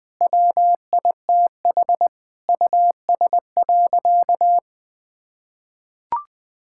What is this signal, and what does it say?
Morse code, heard as a keyed tone, about 20 wpm, 700 Hz. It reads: WITH US.
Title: Sentences with 5-Word Phrases - Spoken Beginning - 20wpm